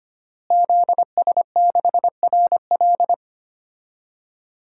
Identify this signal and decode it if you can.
Morse code, heard as a keyed tone, about 25 wpm, 700 Hz. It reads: ZS6RL
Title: DXCC-100 Call Signs - 5 Characters - Repeated 3x - 25wpm